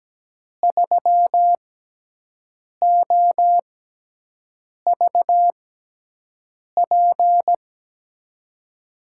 Morse code, heard as a keyed tone, about 17 wpm, spaced as for 6 wpm, 700 Hz. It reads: 3OVP
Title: USA Call Signs - 5 Characters - 6x Character Spacing - 17wpm